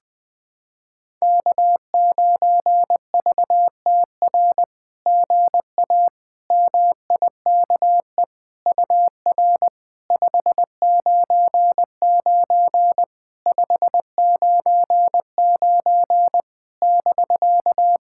Morse code, heard as a keyed tone, about 20 wpm, 700 Hz. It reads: K9VTR GA MIKE UR 599 599 <BK>
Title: SOTA QSO Practice 20wpm